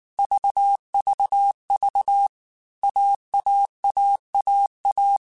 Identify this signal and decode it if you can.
Morse code, heard as a keyed tone, about 19 wpm, 800 Hz. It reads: VVV AAAAA
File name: Groups@18 / Groups@18 mp3 / AWUNDG@18